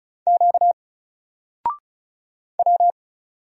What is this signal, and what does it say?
Morse code, heard as a keyed tone, about 36 wpm, 700 Hz. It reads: Q W
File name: Single-Letter-Number-35wpm